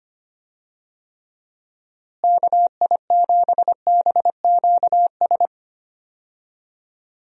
Morse code, heard as a keyed tone, about 25 wpm, 700 Hz. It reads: KI7BQS
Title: USA Call Signs - 6 Characters - Repeated 3x - 25wpm